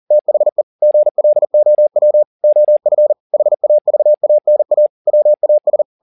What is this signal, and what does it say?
Morse code, heard as a keyed tone, about 40 wpm, 600 Hz. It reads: THE GLOW OF HAVANA WAS